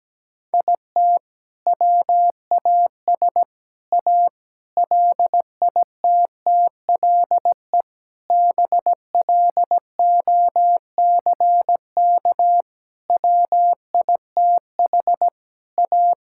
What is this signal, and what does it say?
Morse code, heard as a keyed tone, about 17 wpm, 700 Hz. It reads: IT WAS A LITTLE BLOCK WITH A
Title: Sentences from Top 1600 Words 17wpm